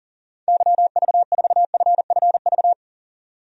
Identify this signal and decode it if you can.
Morse code, heard as a keyed tone, about 37 wpm, 700 Hz. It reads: YV4FFV
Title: DXCC-100 Call Signs - 6 Characters - 40wpm